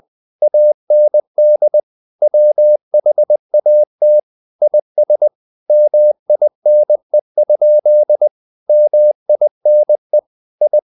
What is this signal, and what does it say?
Morse code, heard as a keyed tone, about 20 wpm, 600 Hz. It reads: AND WHAT IS MINE? MINE I
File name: the old man and the sea / the old man and the sea part 1 20wpm